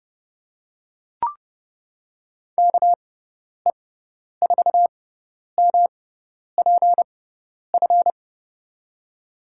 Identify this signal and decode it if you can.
Morse code, heard as a keyed tone, about 30 wpm, spaced as for 10 wpm, 700 Hz. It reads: KE4MPF
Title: USA Call Signs - 6x Character Spacing - 30wpm